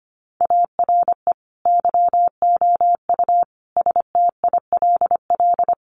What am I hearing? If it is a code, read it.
Morse code, heard as a keyed tone, about 25 wpm, 700 Hz. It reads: ARE YOU STILL
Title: Sentences from Top 400 Words 25wpm